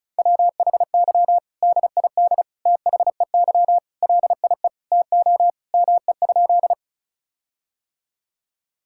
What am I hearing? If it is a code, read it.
Morse code, heard as a keyed tone, about 36 wpm, 700 Hz. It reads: WHY DID THEY LIE TO ME?
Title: Sentences from Top 1600 Words 35wpm